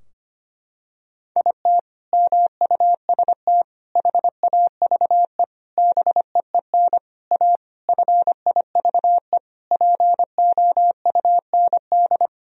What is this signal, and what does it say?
Morse code, heard as a keyed tone, about 25 wpm, 700 Hz. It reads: IT MUST HAVE BEEN A FIVE POUND
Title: Sentences from Top 1000 Words - Review Part 2 - 25wpm